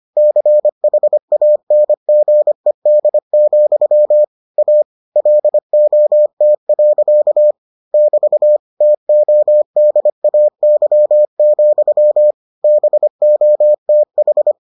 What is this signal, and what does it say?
Morse code, heard as a keyed tone, about 25 wpm, 600 Hz. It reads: CHANGED, A LOT. <BT> TODAY, BOTH